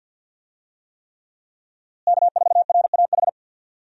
Morse code, heard as a keyed tone, about 49 wpm, 700 Hz. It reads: K4RAH
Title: USA Call Signs - 5 Characters - Repeated 3x - 50wpm